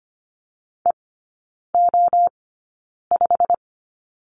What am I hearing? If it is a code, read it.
Morse code, heard as a keyed tone, about 25 wpm, spaced as for 9 wpm, 690 Hz. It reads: EO5
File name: Lesson 15e - Number 5 - Three Characters 25wpm